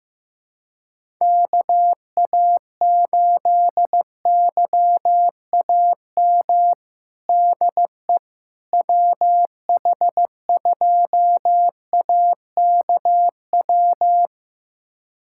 Morse code, heard as a keyed tone, about 15 wpm, 700 Hz. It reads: KA8YAM DE WH2AKW
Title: USA Callsign DE Callsign - 6 Characters - 15wpm